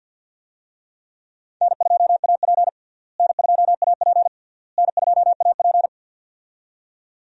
Morse code, heard as a keyed tone, about 51 wpm, 700 Hz. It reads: N2AP N2AP N2AP
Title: USA Call Signs - 4 Characters - Repeated 3x - 50wpm